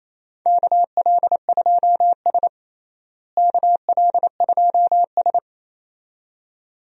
Morse code, heard as a keyed tone, about 28 wpm, 700 Hz. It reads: KL2S KL2S
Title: USA Call Signs - 4 Characters - Repeated 2x - 28wpm